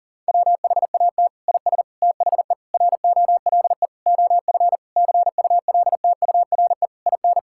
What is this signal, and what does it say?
Morse code, heard as a keyed tone, about 40 wpm, 700 Hz. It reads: WHAT IS THE ROLE OF CULTURE IN
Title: Sentences from Top 1000 Words - Review Part 6 - 40wpm